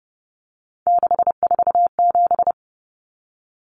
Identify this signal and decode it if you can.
Morse code, heard as a keyed tone, about 30 wpm, 700 Hz. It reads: 647
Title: Numbers - 3-Digit 30wpm